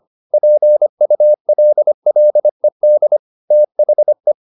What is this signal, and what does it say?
Morse code, heard as a keyed tone, about 25 wpm, 600 Hz. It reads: PULLED THE